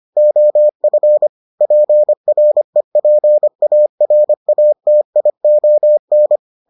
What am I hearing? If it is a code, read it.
Morse code, heard as a keyed tone, about 25 wpm, 600 Hz. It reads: OF PREPARATION